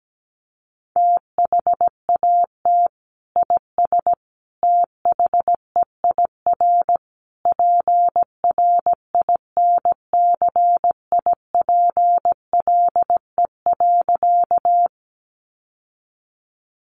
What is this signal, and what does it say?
Morse code, heard as a keyed tone, about 17 wpm, 700 Hz. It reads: THAT IS THEIR PRINCIPLE.
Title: Sentences from Top 1400 Words 17wpm